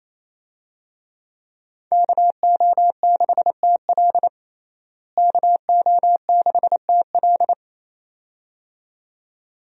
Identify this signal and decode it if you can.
Morse code, heard as a keyed tone, about 27 wpm, 700 Hz. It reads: KO6TL KO6TL
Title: USA Call Signs - 5 Characters - Repeated 2x - 28wpm